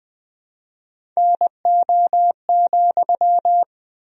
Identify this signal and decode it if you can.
Morse code, heard as a keyed tone, about 20 wpm, 700 Hz. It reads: NO,
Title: 4x Word-Spacing - Sentences from Top 300 Words - 20wpm